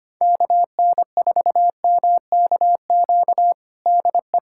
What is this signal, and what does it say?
Morse code, heard as a keyed tone, about 25 wpm, 700 Hz. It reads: KN4MKQ DE
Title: USA Callsign DE Callsign - 6 Characters - 25wpm